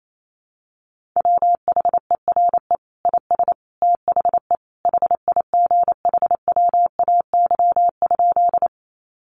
Morse code, heard as a keyed tone, about 28 wpm, 700 Hz. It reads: WHERE IS THE HIGHWAY?
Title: Sentences from Top 1400 Words 28wpm